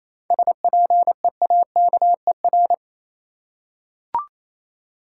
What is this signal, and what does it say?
Morse code, heard as a keyed tone, about 28 wpm, 700 Hz. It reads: SPEAKER
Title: Seven Letter Words - 28wpm